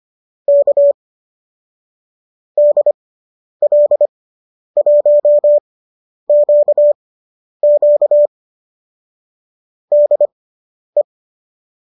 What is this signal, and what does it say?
Morse code, heard as a keyed tone, about 25 wpm, spaced as for 10 wpm, 590 Hz. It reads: K DL1QQ DE
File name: qso219 10